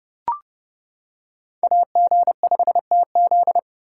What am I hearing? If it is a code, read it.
Morse code, heard as a keyed tone, about 28 wpm, 700 Hz. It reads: AG5TZ